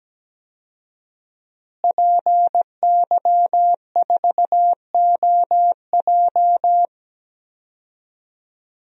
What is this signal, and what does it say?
Morse code, heard as a keyed tone, about 17 wpm, 700 Hz. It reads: PY4OJ
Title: DXCC-100 Call Signs - 5 Characters - Repeated 3x - 17wpm